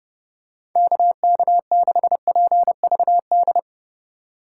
Morse code, heard as a keyed tone, about 30 wpm, 700 Hz. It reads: KK6PVD
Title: USA Call Signs - 6 Characters - 4x Character Spacing - 30wpm